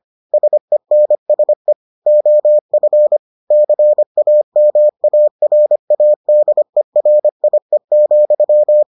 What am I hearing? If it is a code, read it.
Morse code, heard as a keyed tone, about 25 wpm, 600 Hz. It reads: SENSE OF CAMARADERIE,